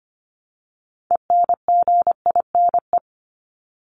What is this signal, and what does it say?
Morse code, heard as a keyed tone, about 25 wpm, 700 Hz. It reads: ENGINE
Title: Top 1251-1500 Words 25wpm